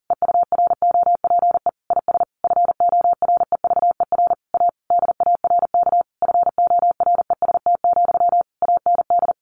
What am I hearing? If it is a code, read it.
Morse code, heard as a keyed tone, about 37 wpm, 700 Hz. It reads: EUROPE IS FOREVER A DARK FOREST, AND
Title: A Sound Of Thunder 40wpm